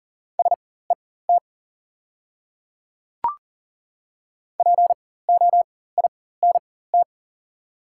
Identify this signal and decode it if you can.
Morse code, heard as a keyed tone, about 40 wpm, spaced as for 19 wpm, 690 Hz. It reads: SET POINT
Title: 4x Character Spacing - Top 300 Words - 40wpm